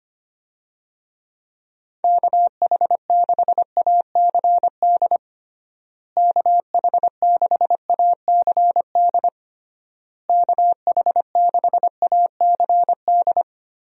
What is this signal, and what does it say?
Morse code, heard as a keyed tone, about 25 wpm, 700 Hz. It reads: KH6ACD KH6ACD KH6ACD
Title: USA Call Signs - 6 Characters - Repeated 3x - 25wpm